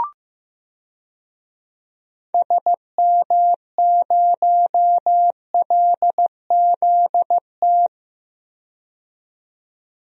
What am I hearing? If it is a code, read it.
Morse code, heard as a keyed tone, about 15 wpm, 700 Hz. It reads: SM0LZT
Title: DXCC-100 Call Signs - 6 Characters - 15wpm